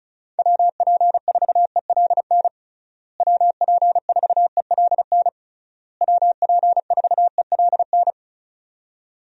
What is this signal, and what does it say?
Morse code, heard as a keyed tone, about 34 wpm, 700 Hz. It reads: WP4ELN WP4ELN WP4ELN
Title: USA Call Signs - 6 Characters - Repeated 3x - 35wpm